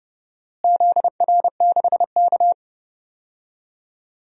Morse code, heard as a keyed tone, about 32 wpm, 700 Hz. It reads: ZR6K
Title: DXCC-100 Call Signs - 4 Characters - Repeated 3x - 30wpm